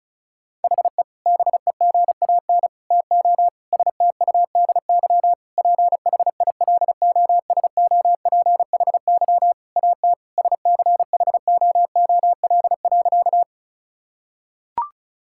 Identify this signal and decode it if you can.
Morse code, heard as a keyed tone, about 36 wpm, 700 Hz. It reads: HE BEGAN TO STUDY PHILOSOPHY AT SCHOOL.
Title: Sentences from Top 1000 Words - Review Part 6 - 35wpm